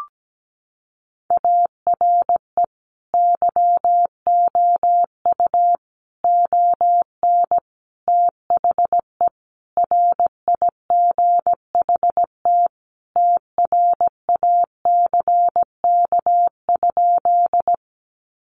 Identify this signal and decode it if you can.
Morse code, heard as a keyed tone, about 17 wpm, 700 Hz. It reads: ARE YOU ON THE RIGHT TRACK?
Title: Sentences from Top 1100 Words 17wpm